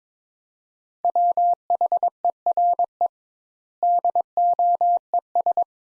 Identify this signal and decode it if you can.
Morse code, heard as a keyed tone, about 22 wpm, 700 Hz. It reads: WHERE DOES
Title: Sentences Top 200 Words - 22wpm